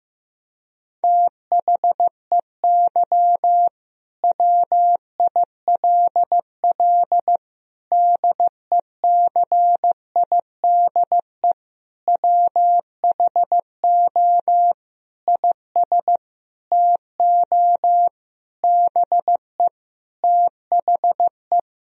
Morse code, heard as a keyed tone, about 16 wpm, 700 Hz. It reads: THEY WILL DECIDE WHO IS TO BE THE
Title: Sentences from Top 1000 Words - Review Part 3 - 15wpm